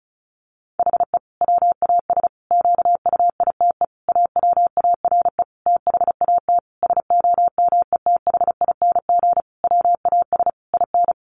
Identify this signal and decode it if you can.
Morse code, heard as a keyed tone, about 34 wpm, 700 Hz. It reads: HE WAS QUITE AWARE THAT SOMETHING WAS IN
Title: Sentences from Top 1400 Words 35wpm